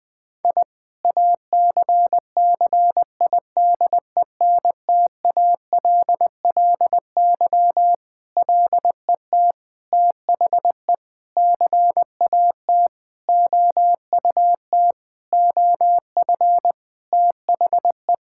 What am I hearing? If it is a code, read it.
Morse code, heard as a keyed tone, about 21 wpm, 700 Hz. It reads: I ACCIDENTALLY LET THE CAT OUT OF THE
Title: Sentences with 7-Word Phrases - 20wpm